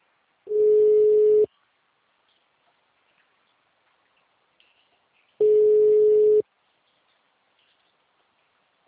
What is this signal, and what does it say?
Ringback tone (Europe, 425 Hz).